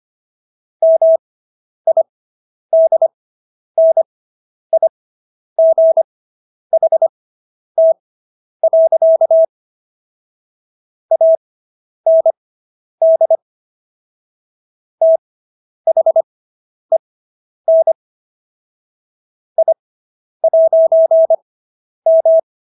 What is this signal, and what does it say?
Morse code, heard as a keyed tone, about 25 wpm, spaced as for 10 wpm, 650 Hz. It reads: MIDNIGHT. AND THEN I'M